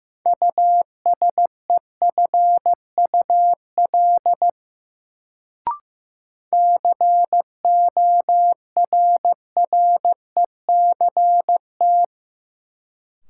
Morse code, heard as a keyed tone, about 15 wpm, 700 Hz. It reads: USEFUL CORRECT
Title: Sentences from Top 700 Words - New Word Drill - 15wpm